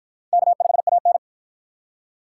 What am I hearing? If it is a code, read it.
Morse code, heard as a keyed tone, about 53 wpm, 700 Hz. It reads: K5AN